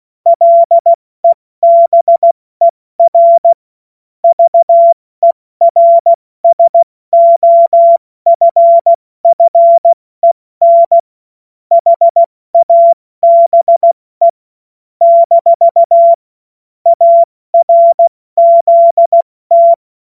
Morse code, heard as a keyed tone, about 16 wpm, 680 Hz. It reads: LEBER VERSOFFEN HABE - ARZT
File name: Morsekurs / Witze / Witz 22. Juni 2021, 70 F80